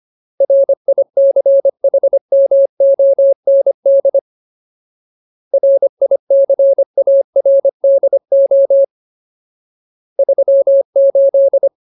Morse code, heard as a keyed tone, about 25 wpm, 550 Hz. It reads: RICHMOND RICARDO 38
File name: Ultimate Word List / 25WPM Ultimate